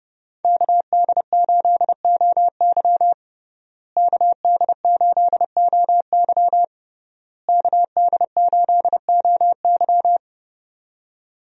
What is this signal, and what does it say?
Morse code, heard as a keyed tone, about 32 wpm, 700 Hz. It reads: KD8OY KD8OY KD8OY
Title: USA Call Signs - 5 Characters - Repeated 3x - 30wpm